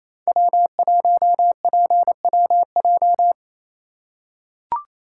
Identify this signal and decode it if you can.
Morse code, heard as a keyed tone, about 28 wpm, 700 Hz. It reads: W1PWJ
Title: USA Call Signs - 5 Characters - 2x Character Spacing - 28wpm